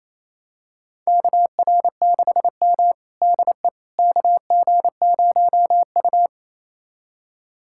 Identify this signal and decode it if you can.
Morse code, heard as a keyed tone, about 28 wpm, 700 Hz. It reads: KR6M DE KG0U
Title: USA Callsign DE Callsign - 4 Characters - 28wpm